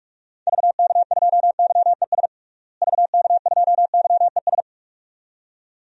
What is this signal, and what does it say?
Morse code, heard as a keyed tone, about 46 wpm, 700 Hz. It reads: VK2YES VK2YES
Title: DXCC-100 Call Signs - 6 Characters - Repeated 2x - 45wpm